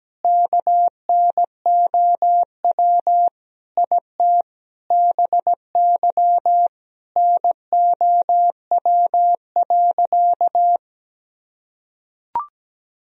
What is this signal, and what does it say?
Morse code, heard as a keyed tone, about 17 wpm, 700 Hz. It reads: KNOW IT BY NOW.
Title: Sentences from Top 1000 Words - Review Part 2 - 17wpm